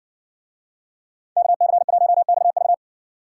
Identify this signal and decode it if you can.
Morse code, heard as a keyed tone, about 60 wpm, 700 Hz. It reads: KC1XV